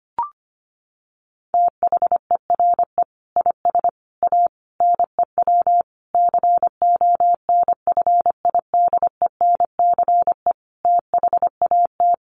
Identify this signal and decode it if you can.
Morse code, heard as a keyed tone, about 25 wpm, 700 Hz. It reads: THERE IS A NEW CONFIDENCE THAT
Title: Sentences from Top 1500 Words 25wpm